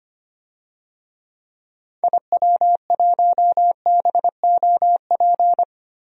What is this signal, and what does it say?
Morse code, heard as a keyed tone, about 25 wpm, 700 Hz. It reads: IW1BOP
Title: DXCC-100 Call Signs - 6 Characters - Repeated 2x - 25wpm